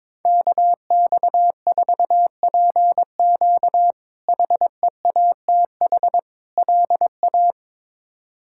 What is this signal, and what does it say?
Morse code, heard as a keyed tone, about 22 wpm, 700 Hz. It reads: KX4PQ HEATH LA
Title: K1USN SST QSO Practice - 22wpm